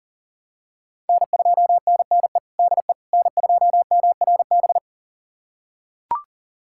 Morse code, heard as a keyed tone, about 40 wpm, 700 Hz. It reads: N2NNE DE N2MRB
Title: USA Callsign DE Callsign - 5 Characters - 40wpm